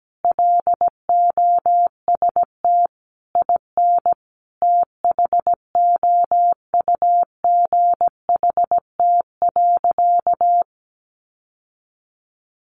Morse code, heard as a keyed tone, about 17 wpm, 700 Hz. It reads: LOST IN THOUGHT.
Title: Sentences from Top 1400 Words 17wpm